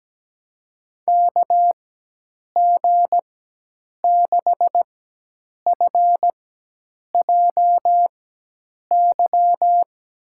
Morse code, heard as a keyed tone, about 17 wpm, spaced as for 8 wpm, 700 Hz. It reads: KG6FJY
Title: USA Call Signs - 6 Characters - 4x Character Spacing - 17wpm